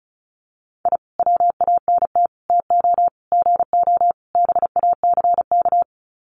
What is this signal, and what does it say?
Morse code, heard as a keyed tone, about 34 wpm, 700 Hz. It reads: I WANT TO GO BACK
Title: Sets of 5 Words Encore 35wpm